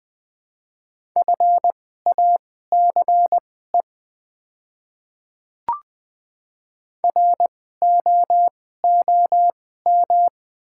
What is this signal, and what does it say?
Morse code, heard as a keyed tone, about 21 wpm, spaced as for 15 wpm, 700 Hz. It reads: FACE ROOM
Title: 2x Character Spacing - Top 300 Words - 20wpm